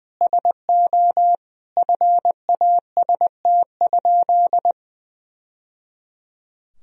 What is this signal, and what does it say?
Morse code, heard as a keyed tone, about 20 wpm, 700 Hz. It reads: SO FAST?
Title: Sentences from Top 1000 Words - Review Part 2 - 20wpm